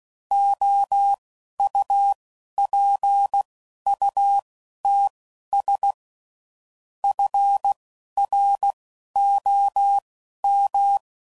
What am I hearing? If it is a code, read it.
Morse code, heard as a keyed tone, about 16 wpm, spaced as for 11 wpm, 800 Hz. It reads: OUPUTS FROM
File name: CW Code Groups @10WPM / Technical Subjects / SwPwrSup